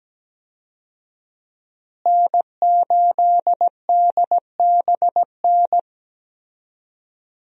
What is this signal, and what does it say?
Morse code, heard as a keyed tone, about 17 wpm, 700 Hz. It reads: N8DBN
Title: USA Call Signs - 5 Characters - Repeated 2x - 17wpm